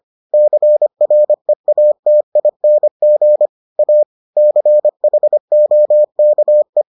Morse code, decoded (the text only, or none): CREATING A CHOKE